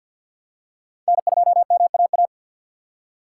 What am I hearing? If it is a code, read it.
Morse code, heard as a keyed tone, about 46 wpm, 700 Hz. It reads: N2MAA
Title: USA Call Signs - 5 Characters - 4x Character Spacing - 50wpm